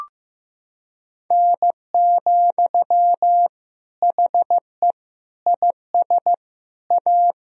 Morse code, heard as a keyed tone, about 15 wpm, 700 Hz. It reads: N, HE IS A
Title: Sentences from Top 400 Words and Abbreviations 15wpm